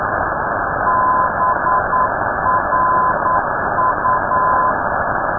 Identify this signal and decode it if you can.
Morse code, heard as a keyed tone, about 9 wpm, 1020 Hz. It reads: BRU
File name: europe / d-3079-cw-id-2005-rb